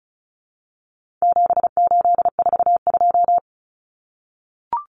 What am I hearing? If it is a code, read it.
Morse code, heard as a keyed tone, about 34 wpm, 700 Hz. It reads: 7842
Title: Numbers - 4-Digit 35wpm